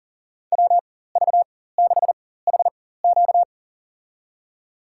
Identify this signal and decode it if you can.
Morse code, heard as a keyed tone, about 40 wpm, spaced as for 19 wpm, 700 Hz. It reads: WV6HQ
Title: USA Call Signs - 5 Characters - 4x Character Spacing - 40wpm